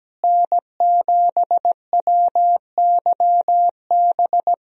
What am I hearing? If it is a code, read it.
Morse code, heard as a keyed tone, about 17 wpm, 700 Hz. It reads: N7WYB
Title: USA Call Signs - 5 Characters - Repeated 3x - 17wpm